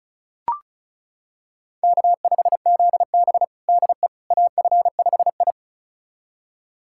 Morse code, heard as a keyed tone, about 36 wpm, 700 Hz. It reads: K5ZB DE AF5I
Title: USA Callsign DE Callsign - 4 Characters - 35wpm